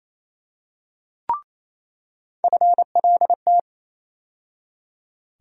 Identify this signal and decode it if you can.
Morse code, heard as a keyed tone, about 28 wpm, 700 Hz. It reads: FLT
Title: USA Call Sign Postfixes - 3 Letters 28wpm